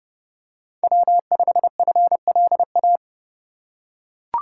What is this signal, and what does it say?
Morse code, heard as a keyed tone, about 28 wpm, 700 Hz. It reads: W5FLA